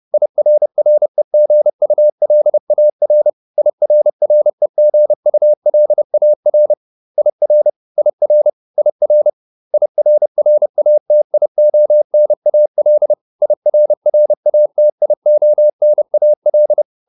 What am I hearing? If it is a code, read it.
Morse code, heard as a keyed tone, about 30 wpm, 600 Hz. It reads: IRREGULAR IRREGULAR IR IR IR IRRATIONAL IRRATIONAL